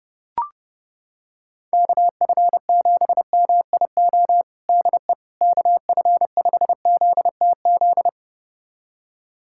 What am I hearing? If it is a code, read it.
Morse code, heard as a keyed tone, about 32 wpm, 700 Hz. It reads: KF7MIO DE KF5ZTZ
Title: USA Callsign DE Callsign 30wpm